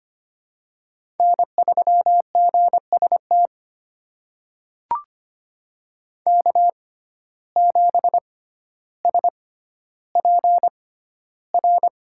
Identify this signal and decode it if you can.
Morse code, heard as a keyed tone, about 25 wpm, spaced as for 13 wpm, 690 Hz. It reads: N3GST K7SPR
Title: USA Call Signs - 5 Characters - 6x Character Spacing - 25wpm